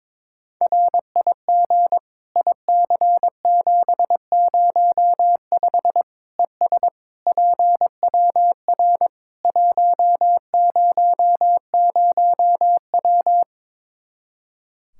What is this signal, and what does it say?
Morse code, heard as a keyed tone, about 21 wpm, 700 Hz. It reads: RIG IC705 ES PWR 100W
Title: Practice QSO - Exchange 2 - Info - 22wpm